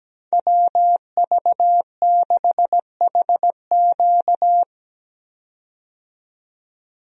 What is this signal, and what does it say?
Morse code, heard as a keyed tone, about 17 wpm, 700 Hz. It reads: WV6HQ
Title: USA Call Signs - 5 Characters - 17wpm